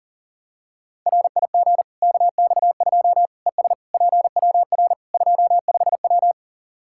Morse code, heard as a keyed tone, about 40 wpm, 700 Hz. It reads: RIG KX2 ES PWR 25W
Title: Practice QSO - Exchange 2 - Info - 40wpm